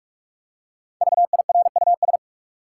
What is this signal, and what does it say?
Morse code, heard as a keyed tone, about 46 wpm, 700 Hz. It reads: VIRUS
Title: Five Letter Words - 45wpm